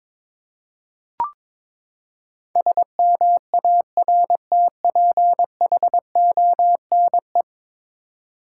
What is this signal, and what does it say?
Morse code, heard as a keyed tone, about 22 wpm, 700 Hz. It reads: SMARTPHONE
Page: morse-code-ninja-practice-22wpm